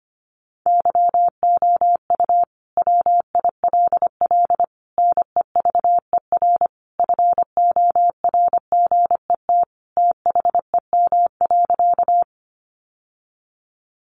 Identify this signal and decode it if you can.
Morse code, heard as a keyed tone, about 25 wpm, 700 Hz. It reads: YOU WILL NEVER FORGET THEM.
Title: Sentences from Top 1200 Words 25wpm